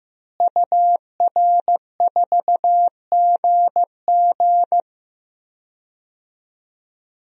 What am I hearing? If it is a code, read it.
Morse code, heard as a keyed tone, about 15 wpm, 700 Hz. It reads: UR4GG
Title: DXCC-100 Call Signs - 5 Characters - Repeated 3x - 15wpm